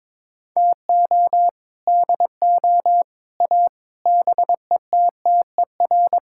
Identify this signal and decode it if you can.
Morse code, heard as a keyed tone, about 22 wpm, 700 Hz. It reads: TO DO A BETTER
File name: practice / 22wpm / Sets of 6 Words - 22wpm